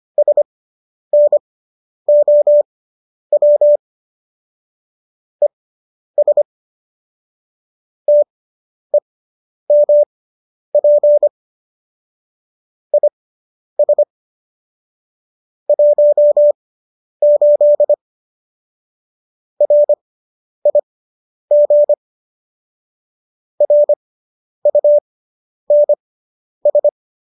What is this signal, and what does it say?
Morse code, heard as a keyed tone, about 25 wpm, spaced as for 10 wpm, 590 Hz. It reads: SNOW ES TEMP IS 18 RIG RUNS